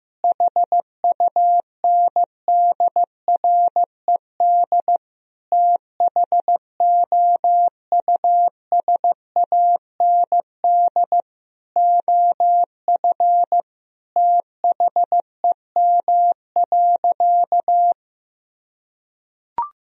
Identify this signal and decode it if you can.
Morse code, heard as a keyed tone, about 16 wpm, 700 Hz. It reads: HUNDRED THOUSAND OF THEM.
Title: Sentences from Top 1000 Words - Review Part 3 - 15wpm